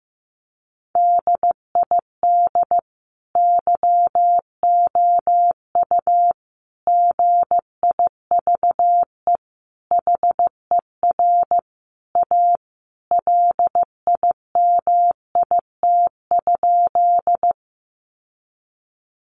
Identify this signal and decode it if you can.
Morse code, heard as a keyed tone, about 15 wpm, 700 Hz. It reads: DID YOU GIVE HER A LIMIT?
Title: Sentences from Top 1500 Words 15wpm